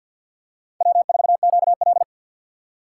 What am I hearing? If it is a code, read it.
Morse code, heard as a keyed tone, about 48 wpm, 700 Hz. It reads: W4QL
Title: USA Call Signs - 4 Characters - 3x Character Spacing - 50wpm